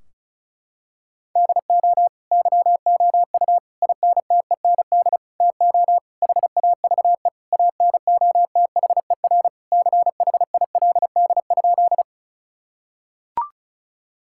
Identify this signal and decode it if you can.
Morse code, heard as a keyed tone, about 34 wpm, 700 Hz. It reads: DO YOU INTEND TO HAVE ANOTHER CHILD?
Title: Sentences from Top 1600 Words 35wpm